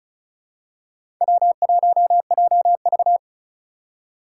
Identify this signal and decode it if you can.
Morse code, heard as a keyed tone, about 36 wpm, 700 Hz. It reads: W1JV
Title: USA Call Signs - 4 Characters - 2x Character Spacing - 35wpm